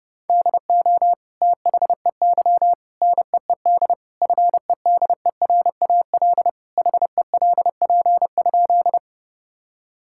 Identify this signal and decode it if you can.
Morse code, heard as a keyed tone, about 30 wpm, 700 Hz. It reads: DO THEY NEED FEDERAL HELP?
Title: Sentences from Top 1000 Words - Review Part 5 - 30wpm